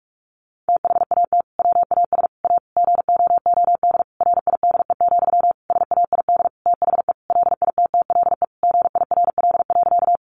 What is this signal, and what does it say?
Morse code, heard as a keyed tone, about 44 wpm, 700 Hz. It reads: THAT WAS A GOOD RIDE, SAID THE LITTLE GIRL.
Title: Sentences from Top 1200 Words 45wpm